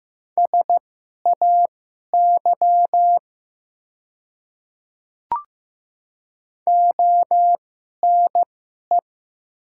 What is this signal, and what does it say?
Morse code, heard as a keyed tone, about 15 wpm, spaced as for 11 wpm, 700 Hz. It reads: SAY ONE
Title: 2x Character Spacing - Top 300 Words - 15wpm